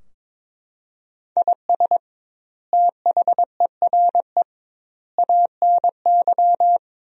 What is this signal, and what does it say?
Morse code, heard as a keyed tone, about 22 wpm, 700 Hz. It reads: IS THERE ANY
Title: 2x Word-Spacing - Sentences from Top 300 Words - 22wpm